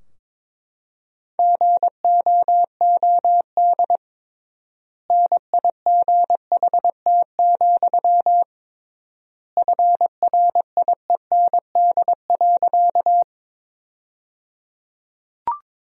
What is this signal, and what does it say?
Morse code, heard as a keyed tone, about 22 wpm, 700 Hz. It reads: GOOD NIGHT, FRIEND.
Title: 3x Word-Spacing - Sentences from Top 300 Words - 22wpm